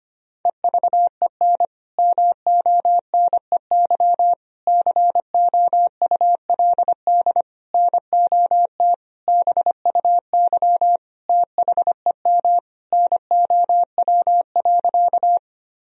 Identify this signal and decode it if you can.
Morse code, heard as a keyed tone, about 25 wpm, 700 Hz. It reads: EVEN MONEY COULD NOT BUY THEM NOW.
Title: Sentences from Top 1000 Words - Review Part 4 - 25wpm